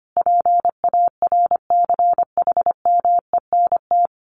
Morse code, heard as a keyed tone, about 25 wpm, 700 Hz. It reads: PARCHMENT
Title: A Sound Of Thunder 25wpm